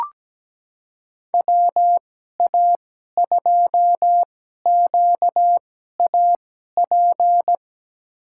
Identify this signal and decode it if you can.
Morse code, heard as a keyed tone, about 17 wpm, spaced as for 12 wpm, 700 Hz. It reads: WA2QAP
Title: USA Call Signs - 6 Characters - 2x Character Spacing - 17wpm